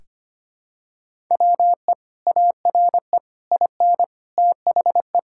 Morse code, heard as a keyed tone, about 25 wpm, 700 Hz. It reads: WE ARE IN THE
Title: Sentences from Top 1000 Words - Review Part 6 - 25wpm